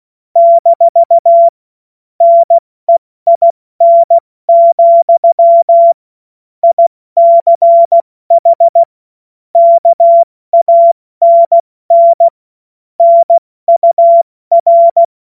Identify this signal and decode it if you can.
Morse code, heard as a keyed tone, about 16 wpm, 680 Hz. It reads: - NEIN, ICH KANN NUR
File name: Morsekurs / Witze / Witz 02. Dezember 2025 70 F80